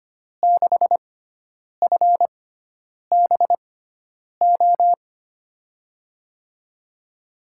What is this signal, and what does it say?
Morse code, heard as a keyed tone, about 25 wpm, spaced as for 9 wpm, 700 Hz. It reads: 6FBO
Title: USA Call Signs - 6x Character Spacing - 25wpm